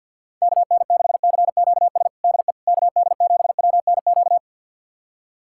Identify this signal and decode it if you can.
Morse code, heard as a keyed tone, about 51 wpm, 690 Hz. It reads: KN6CQS DE KD7WNQ
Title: USA Callsign DE Callsign - 6 Characters - 50wpm